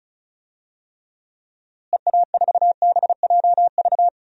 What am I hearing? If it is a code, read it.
Morse code, heard as a keyed tone, about 36 wpm, 700 Hz. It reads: EA4BJV
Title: DXCC-100 Call Signs - 6 Characters - Repeated 3x - 35wpm